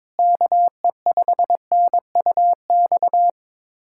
Morse code, heard as a keyed tone, about 22 wpm, 700 Hz. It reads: KE5NUX
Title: USA Callsign DE Callsign 22wpm